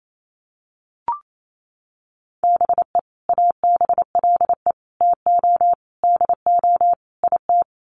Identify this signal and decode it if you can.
Morse code, heard as a keyed tone, about 28 wpm, 700 Hz. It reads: BE ABLE TO DO IT
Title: Sets of 5 Words Encore2 28wpm